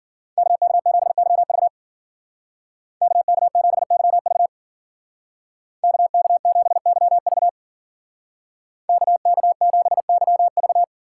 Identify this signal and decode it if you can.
Morse code, heard as a keyed tone, about 52 wpm, 700 Hz. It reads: KK7YV KK7YV KK7YV KK7YV